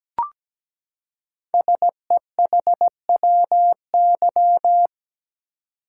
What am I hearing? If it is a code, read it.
Morse code, heard as a keyed tone, about 17 wpm, 700 Hz. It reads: SEHWY
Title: Code Groups - 5 Letters - 17wpm